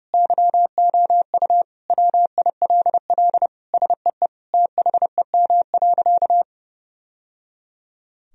Morse code, heard as a keyed tone, about 30 wpm, 700 Hz. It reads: YOU WILL SEE THEM.
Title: Sentences from Top 1000 Words - Review Part 2 - 30wpm